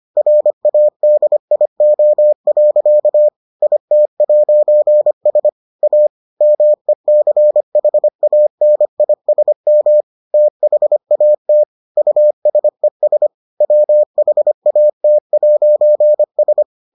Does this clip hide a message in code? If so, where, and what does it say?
Yes: Morse code from 0.2 to 16.6 s, heard as a keyed tone, about 25 wpm, 600 Hz. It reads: RADIO. IT'S A MECHANISM THAT USES WHAT'S